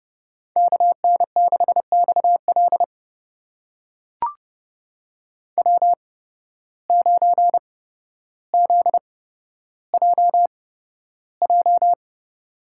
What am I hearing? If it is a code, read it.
Morse code, heard as a keyed tone, about 30 wpm, spaced as for 13 wpm, 700 Hz. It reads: KN6XL W9ZJJ
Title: USA Call Signs - 5 Characters - 8x Character Spacing - 30wpm